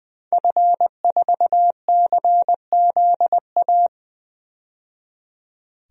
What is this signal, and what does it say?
Morse code, heard as a keyed tone, about 20 wpm, 700 Hz. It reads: F4CZA
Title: DXCC-100 Call Signs - 5 Characters - 20wpm